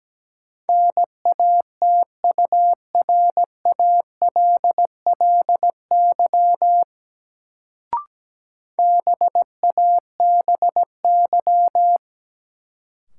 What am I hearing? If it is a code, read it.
Morse code, heard as a keyed tone, about 17 wpm, 700 Hz. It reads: NATURALLY BABY
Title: Sentences from Top 1300 Words - New Word Drill Alt - 17wpm